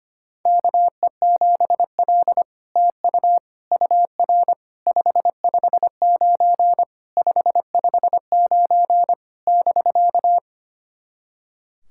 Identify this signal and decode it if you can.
Morse code, heard as a keyed tone, about 25 wpm, 700 Hz. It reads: KE7L TU UR 559 559 <BK>
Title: SOTA QSO Practice 25wpm